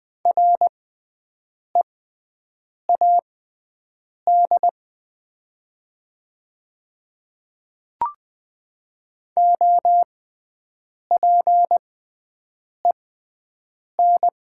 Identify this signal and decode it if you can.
Morse code, heard as a keyed tone, about 20 wpm, spaced as for 7 wpm, 700 Hz. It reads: READ OPEN
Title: 6x Character Spacing - Top 300 Words - 20wpm